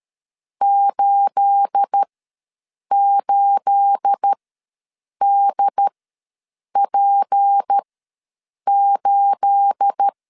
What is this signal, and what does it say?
Morse code, heard as a keyed tone, about 13 wpm, spaced as for 7 wpm, 800 Hz. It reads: 88DP8
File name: LEZIONI / 9 28 ws 12wpm 3354